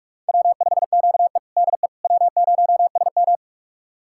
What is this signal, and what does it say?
Morse code, heard as a keyed tone, about 46 wpm, 700 Hz. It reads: W5QE DE W0SM
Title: USA Callsign DE Callsign - 4 Characters - 45wpm